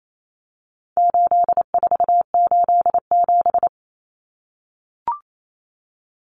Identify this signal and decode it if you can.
Morse code, heard as a keyed tone, about 28 wpm, 700 Hz. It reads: 8487
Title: Common Cut Numbers - 4-Digit 28wpm